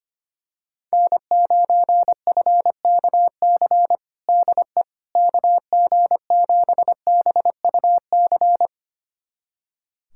Morse code, heard as a keyed tone, about 25 wpm, 700 Hz. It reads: N9FKC DE KG7BUC
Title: USA Callsign DE Callsign 25wpm